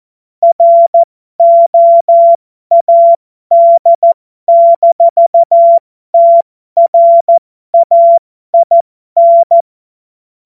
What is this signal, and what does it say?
Morse code, heard as a keyed tone, about 14 wpm, spaced as for 12 wpm, 680 Hz. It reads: ROAD-TRAIN
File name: Morsekurs / Witze